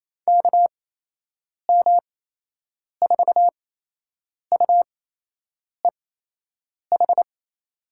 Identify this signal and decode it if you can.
Morse code, heard as a keyed tone, about 28 wpm, spaced as for 8 wpm, 700 Hz. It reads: KM4UEH
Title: USA Call Signs - 6 Characters - 8x Character Spacing - 28wpm